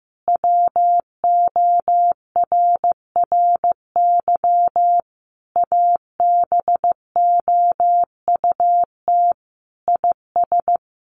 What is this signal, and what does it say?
Morse code, heard as a keyed tone, about 15 wpm, 700 Hz. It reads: WORRY ABOUT IS
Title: A Sound Of Thunder 15wpm